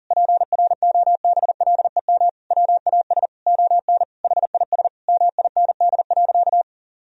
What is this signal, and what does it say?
Morse code, heard as a keyed tone, about 40 wpm, 700 Hz. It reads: PROBLEM WAS ON HIS MIND.
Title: Sentences from Top 1000 Words - Review Part 2 - 40wpm